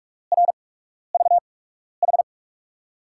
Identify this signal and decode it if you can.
Morse code, heard as a keyed tone, about 44 wpm, spaced as for 12 wpm, 700 Hz. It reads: RVH
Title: USA Call Signs - 8x Character Spacing - 45wpm